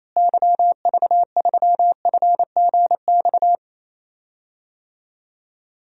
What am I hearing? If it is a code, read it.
Morse code, heard as a keyed tone, about 28 wpm, 700 Hz. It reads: YV3FGX
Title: DXCC-100 Call Signs - 6 Characters - Repeated 2x - 28wpm